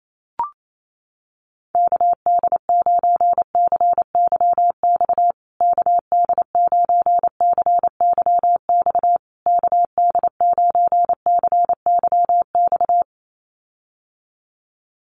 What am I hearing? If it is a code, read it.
Morse code, heard as a keyed tone, about 28 wpm, 700 Hz. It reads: KD9CYX KD9CYX KD9CYX
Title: USA Call Signs Repeated 3x 28wpm